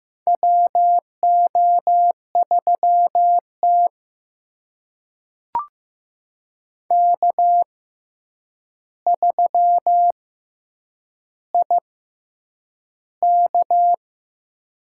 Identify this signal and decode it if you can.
Morse code, heard as a keyed tone, about 15 wpm, spaced as for 8 wpm, 700 Hz. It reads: WO3T K3IK
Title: USA Call Signs - 4 Characters - 6x Character Spacing - 15wpm